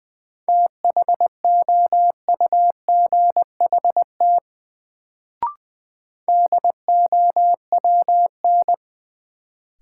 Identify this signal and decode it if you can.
Morse code, heard as a keyed tone, about 20 wpm, 700 Hz. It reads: THOUGHT DOWN
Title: Sentences from Top 300 Words - New Word Drill - 20wpm